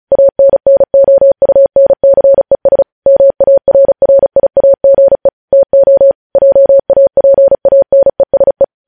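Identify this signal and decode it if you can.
Morse code, heard as a keyed tone, about 36 wpm, 550 Hz. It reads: ANNOUNCES MARRIAGE TO JAPANESE
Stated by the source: Morse code news headlines sent at 35 wpm.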